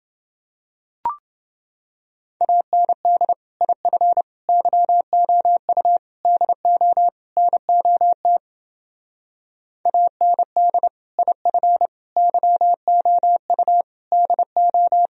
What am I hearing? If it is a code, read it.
Morse code, heard as a keyed tone, about 28 wpm, 700 Hz. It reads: AND IF YOU DO NOT AND IF YOU DO